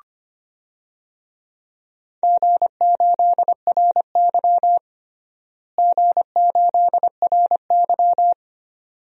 Morse code, heard as a keyed tone, about 25 wpm, 700 Hz. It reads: G8RY G8RY
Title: DXCC-100 Call Signs - 4 Characters - Repeated 2x - 25wpm